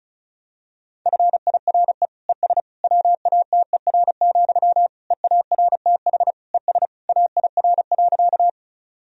Morse code, heard as a keyed tone, about 36 wpm, 700 Hz. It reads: FIRE ES WATER, EARTH ES AIR.
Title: Sentences from Top 400 Words and Abbreviations 35wpm